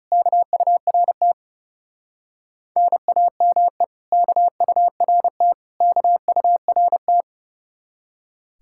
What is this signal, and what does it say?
Morse code, heard as a keyed tone, about 35 wpm, 700 Hz. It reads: KURT NAME KURT KURT
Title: Speed-Racing Common QSO Phrases - 30wpm